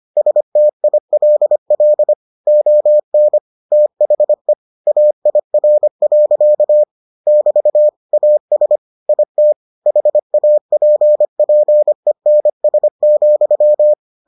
Morse code, heard as a keyed tone, about 25 wpm, 600 Hz. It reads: STILL ON THE AIR. <BT> AS IT HAPPENS,